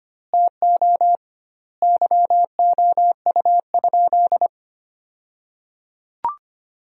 Morse code, heard as a keyed tone, about 25 wpm, 700 Hz. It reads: TO YOU?
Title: 2x Word-Spacing - Sentences from Top 300 Words - 25wpm